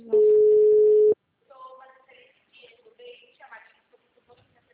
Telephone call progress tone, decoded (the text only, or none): ringback tone